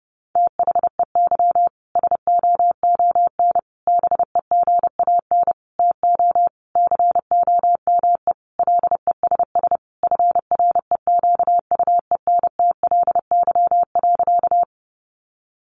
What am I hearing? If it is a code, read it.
Morse code, heard as a keyed tone, about 30 wpm, 700 Hz. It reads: THEY SOON BEGAN TO COME LESS FREQUENTLY.
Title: Sentences from Top 1300 Words 30wpm